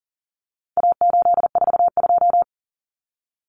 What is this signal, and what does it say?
Morse code, heard as a keyed tone, about 40 wpm, 700 Hz. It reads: A842
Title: Common Cut Numbers - 4-Digit 40wpm